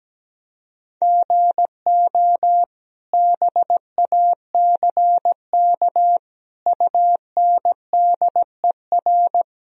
Morse code, heard as a keyed tone, about 17 wpm, 700 Hz. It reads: GO BACK UNDER
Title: Sentences from Top 1000 Words - Review Part 1 - 17wpm